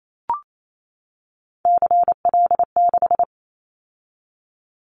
Morse code, heard as a keyed tone, about 28 wpm, 700 Hz. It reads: CL6
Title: Callsign Prefixes for Top 100 DXCC - 28wpm